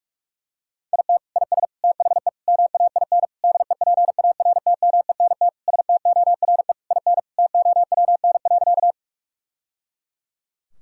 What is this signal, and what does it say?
Morse code, heard as a keyed tone, about 46 wpm, 700 Hz. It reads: IT IS THE MAIN DEPARTMENT STORE IN TOWN.
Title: Sentences from Top 1000 Words - Review Part 5 - 45wpm